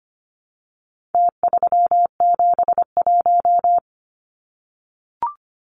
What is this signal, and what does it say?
Morse code, heard as a keyed tone, about 25 wpm, 700 Hz. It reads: T371
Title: Numbers - Contest Style - 4-Digit 25wpm